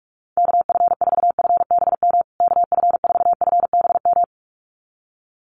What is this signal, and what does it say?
Morse code, heard as a keyed tone, about 44 wpm, 700 Hz. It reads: KF4FBM KF4FBM
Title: USA Call Signs Repeated 2x 45wpm